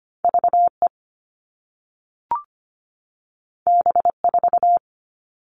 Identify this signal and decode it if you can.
Morse code, heard as a keyed tone, about 25 wpm, 700 Hz. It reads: VE B4
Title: All Cut Numbers - 2-Digit 25wpm